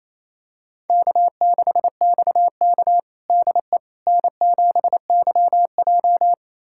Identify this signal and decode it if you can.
Morse code, heard as a keyed tone, about 28 wpm, 700 Hz. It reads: K6XK DE N7YJ
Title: USA Callsign DE Callsign - 4 Characters - 28wpm